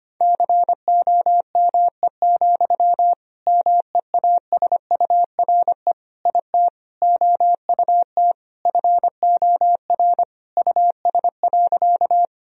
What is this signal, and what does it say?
Morse code, heard as a keyed tone, about 25 wpm, 700 Hz. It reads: COME, MEASURE IT OUT FOR US.
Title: Sentences from Top 1000 Words - Review Part 2 - 25wpm